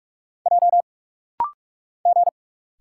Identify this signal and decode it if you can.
Morse code, heard as a keyed tone, about 46 wpm, 700 Hz. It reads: J G
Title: Single Letters - Rapid-Fire 45wpm